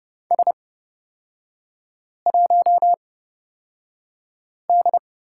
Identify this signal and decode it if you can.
Morse code, heard as a keyed tone, about 30 wpm, spaced as for 5 wpm, 700 Hz. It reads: S1D
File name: Lesson 36e - BT - Three Characters 30wpm